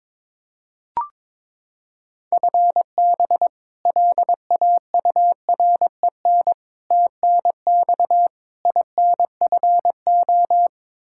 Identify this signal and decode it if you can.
Morse code, heard as a keyed tone, about 22 wpm, 700 Hz. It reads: FB LAUREN TNX INFO
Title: K1USN SST QSO Practice 22wpm